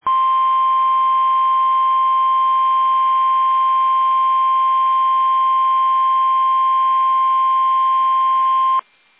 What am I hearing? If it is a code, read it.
Emergency Alert System (SAME) tones (the 1050 Hz attention tone). It is heard from 0.0 to 8.8 s.
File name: eas / KUSO / KUSO